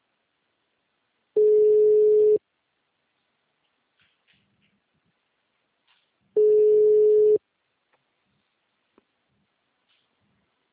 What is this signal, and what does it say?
Ringback tone (Europe, 425 Hz).
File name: out- -2004- - - .5762